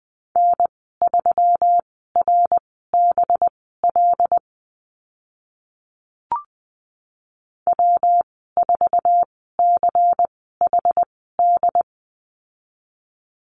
Morse code, heard as a keyed tone, about 20 wpm, spaced as for 15 wpm, 700 Hz. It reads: N3RBL W4CHD
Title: 193 Lesson 31g - Letter v - Call Signs 20wpm